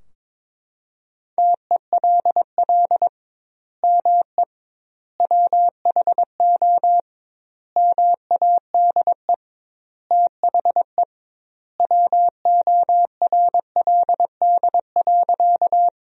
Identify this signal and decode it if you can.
Morse code, heard as a keyed tone, about 21 wpm, 700 Hz. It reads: TELL ME WHO MADE THE WORLD.
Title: 2x Word-Spacing - Sentences from Top 200 Words - 22wpm